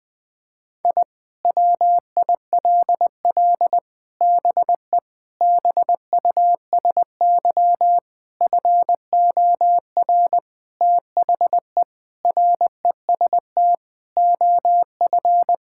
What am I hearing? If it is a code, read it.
Morse code, heard as a keyed tone, about 21 wpm, 700 Hz. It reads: I WILL BE BUSY FOR THE REST OF
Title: Sentences with 5-Word Phrases - Spoken Beginning - 20wpm